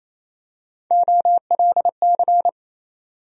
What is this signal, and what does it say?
Morse code, heard as a keyed tone, about 28 wpm, 700 Hz. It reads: OLC
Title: USA Call Sign Postfixes - 3 Letters 28wpm